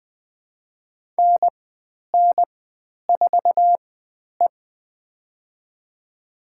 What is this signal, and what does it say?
Morse code, heard as a keyed tone, about 20 wpm, spaced as for 10 wpm, 700 Hz. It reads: NN4E
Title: 095 Lesson 16g - Letter c - Call Signs 20wpm